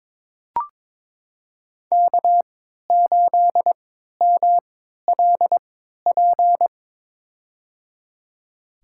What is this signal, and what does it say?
Morse code, heard as a keyed tone, about 21 wpm, spaced as for 12 wpm, 700 Hz. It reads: K8MLP
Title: USA Call Signs - 5 Characters - 3x Character Spacing - 22wpm